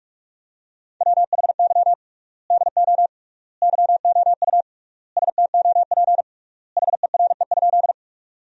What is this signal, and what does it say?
Morse code, heard as a keyed tone, about 46 wpm, 700 Hz. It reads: WHY DO YOU STOP HERE?
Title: 3x Word-Spacing - Sentences from Top 300 Words - 45wpm